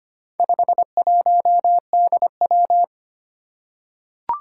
Morse code, heard as a keyed tone, about 25 wpm, 700 Hz. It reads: 51DW
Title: Four Challenging Letter-Number Combinations 25wpm